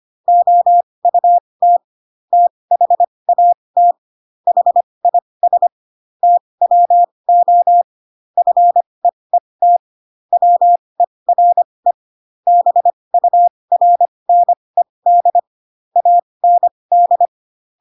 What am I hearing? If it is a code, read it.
Morse code, heard as a keyed tone, about 25 wpm, spaced as for 20 wpm, 700 Hz. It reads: OUT THAT HIS TWO FEET WERE BURNED AND
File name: Pinocchio / Pinocchio ch7 15 25 ik1hge